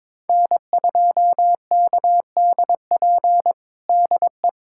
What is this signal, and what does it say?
Morse code, heard as a keyed tone, about 22 wpm, 700 Hz. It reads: N2KDP DE
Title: USA Callsign DE Callsign - 5 Characters - 22wpm